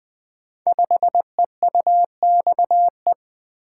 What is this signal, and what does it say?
Morse code, heard as a keyed tone, about 20 wpm, 700 Hz. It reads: 5EUXE
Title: Code Groups - 5 Letter:Numbers - 20wpm